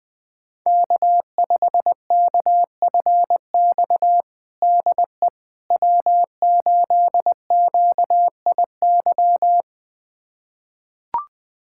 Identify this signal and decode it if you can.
Morse code, heard as a keyed tone, about 20 wpm, 700 Hz. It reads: K5KFX DE W8QIY
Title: USA Callsign DE Callsign - 5 Characters - 20wpm